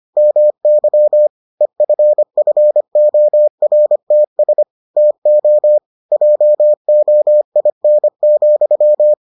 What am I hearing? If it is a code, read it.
Morse code, heard as a keyed tone, about 25 wpm, 600 Hz. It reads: MY EFFORTS TO JOIN,